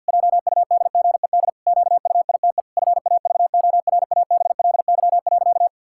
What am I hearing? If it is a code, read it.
Morse code, heard as a keyed tone, about 50 wpm, 700 Hz. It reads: JUDGED QUITE FAVORABLY.
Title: Pangram Challenge v2 - 50wpm